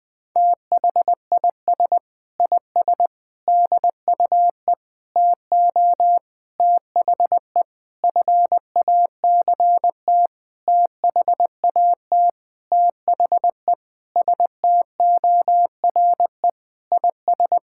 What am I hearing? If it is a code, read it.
Morse code, heard as a keyed tone, about 20 wpm, 700 Hz. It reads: THIS IS DUE TO THE FACT THAT THE STORE IS
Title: Sentences with 7-Word Phrases - Spoken Beginning - 20wpm